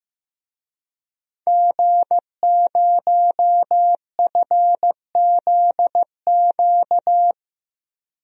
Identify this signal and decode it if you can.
Morse code, heard as a keyed tone, about 15 wpm, 700 Hz. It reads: G0FZQ
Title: DXCC-100 Call Signs - 5 Characters - 15wpm